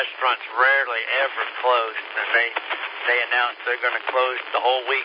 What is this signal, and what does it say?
Emergency Alert System (SAME) tones, heard as harsh buzzing bursts, with speech. It is heard from 0.6 to 4.0 s.